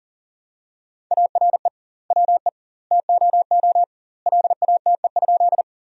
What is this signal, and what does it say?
Morse code, heard as a keyed tone, about 40 wpm, 700 Hz. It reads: ARE WE TOO LATE?
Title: 2x Word-Spacing - Sentences from Top 300 Words - 40wpm